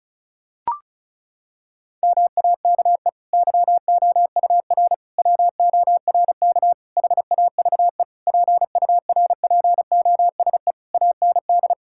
Morse code, heard as a keyed tone, about 34 wpm, 700 Hz. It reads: MAKE YOUR WORK HAVE PURPOSE AND
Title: Sentences from Top 1000 Words - Review Part 6 - 35wpm